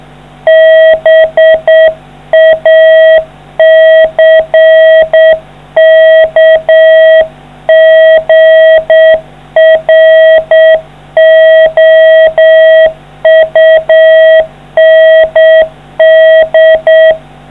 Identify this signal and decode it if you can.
Morse code, heard as a keyed tone, about 7 wpm, 640 Hz. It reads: BACKGROUND